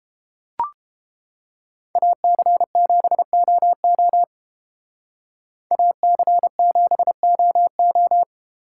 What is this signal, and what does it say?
Morse code, heard as a keyed tone, about 31 wpm, 700 Hz. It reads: AC7OO AC7OO